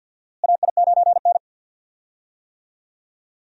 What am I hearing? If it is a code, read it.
Morse code, heard as a keyed tone, about 48 wpm, 700 Hz. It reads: AI9N
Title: USA Call Signs - 4 Characters - Repeated 3x - 50wpm